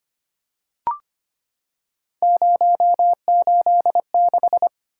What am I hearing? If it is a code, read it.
Morse code, heard as a keyed tone, about 25 wpm, 700 Hz. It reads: 086
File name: Three Letter-Number Combinations - 25wpm